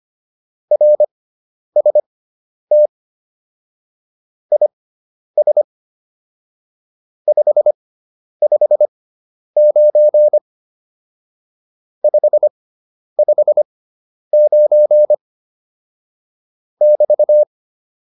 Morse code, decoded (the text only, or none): RST IS 559 559 <BT>